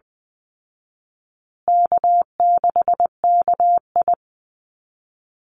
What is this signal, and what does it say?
Morse code, heard as a keyed tone, about 20 wpm, 700 Hz. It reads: K6KI
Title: Easy Call Signs Repeated 3x - 20wpm